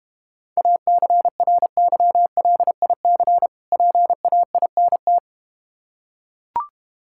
Morse code, heard as a keyed tone, about 32 wpm, 700 Hz. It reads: ACRYLIC PAINT